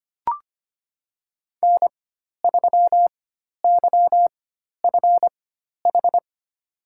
Morse code, heard as a keyed tone, about 25 wpm, spaced as for 12 wpm, 700 Hz. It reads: N3YFH
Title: USA Call Signs - 5 Characters - 4x Character Spacing - 25wpm